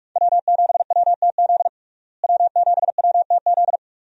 Morse code, heard as a keyed tone, about 48 wpm, 700 Hz. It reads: W7WTZ W7WTZ
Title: USA Call Signs - 5 Characters - Repeated 3x - 45wpm